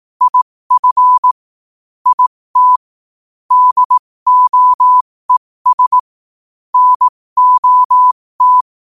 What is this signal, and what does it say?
Morse code, heard as a keyed tone, about 18 wpm, spaced as for 15 wpm, 1000 Hz. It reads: IF IT DOES NOT
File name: qotdfiles / 15wpm